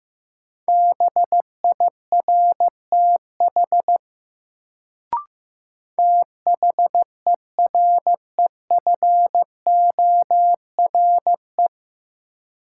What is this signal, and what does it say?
Morse code, heard as a keyed tone, about 15 wpm, 700 Hz. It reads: BIRTH THEREFORE
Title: Sentences from Top 800 Words - New Word Drill - 15wpm